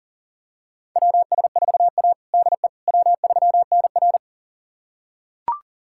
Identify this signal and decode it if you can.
Morse code, heard as a keyed tone, about 40 wpm, 700 Hz. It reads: WS4A DE W3NR
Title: USA Callsign DE Callsign - 4 Characters - 40wpm